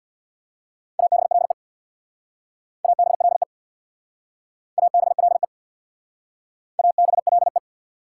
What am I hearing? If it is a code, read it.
Morse code, heard as a keyed tone, about 68 wpm, 690 Hz. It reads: ABLE ABLE ABLE ABLE